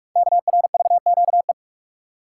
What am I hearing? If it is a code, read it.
Morse code, heard as a keyed tone, about 47 wpm, 700 Hz. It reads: KRVQE
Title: Code Groups - 5 Letters - 45wpm